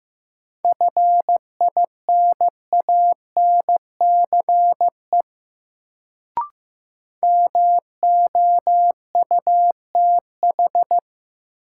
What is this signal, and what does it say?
Morse code, heard as a keyed tone, about 15 wpm, 700 Hz. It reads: FINANCE MOUTH
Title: Sentences from Top 1500 Words - New Word Drill - 15wpm